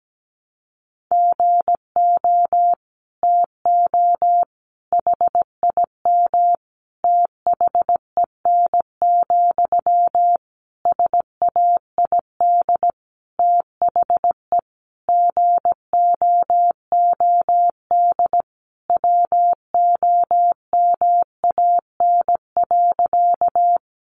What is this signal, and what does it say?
Morse code, heard as a keyed tone, about 17 wpm, 700 Hz. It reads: GO TO HIM THEN, SAID THE GOOD WOMAN.
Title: Sentences from Top 1100 Words 17wpm